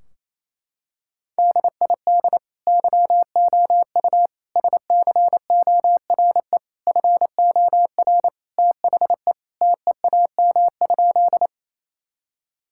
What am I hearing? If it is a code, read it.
Morse code, heard as a keyed tone, about 28 wpm, 700 Hz. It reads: DID YOU SCORE FOR THE TEAM?
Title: Sentences from Top 1600 Words 28wpm